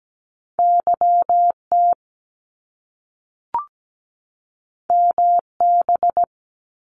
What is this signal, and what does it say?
Morse code, heard as a keyed tone, about 17 wpm, 700 Hz. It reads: YT MB
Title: Canadian Provinces and Territories 17wpm